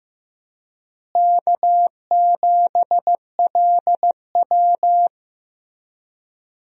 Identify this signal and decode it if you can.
Morse code, heard as a keyed tone, about 15 wpm, 700 Hz. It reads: K7LW
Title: USA Call Signs - 4 Characters - Repeated 3x - 15wpm